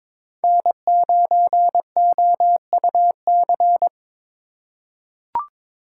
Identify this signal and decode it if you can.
Morse code, heard as a keyed tone, about 22 wpm, 700 Hz. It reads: N9OUC
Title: USA Call Signs - 5 Characters - 4x Character Spacing - 22wpm